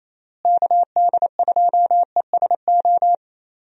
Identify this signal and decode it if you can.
Morse code, heard as a keyed tone, about 28 wpm, 700 Hz. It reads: KD2ESO
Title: USA Call Signs - 6 Characters - Repeated 3x - 28wpm